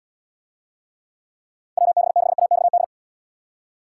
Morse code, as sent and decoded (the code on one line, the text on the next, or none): .-- -... -.... .- -..- -..
WB6AXD